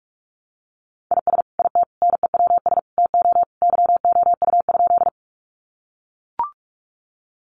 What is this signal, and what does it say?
Morse code, heard as a keyed tone, about 44 wpm, 700 Hz. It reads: IS IT NEWS TO YOU?
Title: Sentences from Top 800 Words 45wpm